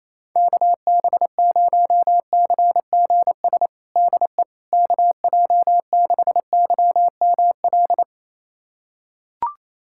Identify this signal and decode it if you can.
Morse code, heard as a keyed tone, about 28 wpm, 700 Hz. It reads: KB0CGS DE KJ6YML
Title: USA Callsign DE Callsign - 6 Characters - 28wpm